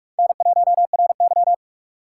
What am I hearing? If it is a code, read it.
Morse code, heard as a keyed tone, about 41 wpm, 700 Hz. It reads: N1RY